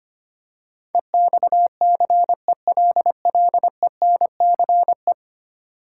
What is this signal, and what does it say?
Morse code, heard as a keyed tone, about 25 wpm, 700 Hz. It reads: EXCELLENCE
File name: 10 Letter Words - First 3 Letters Given - 25wpm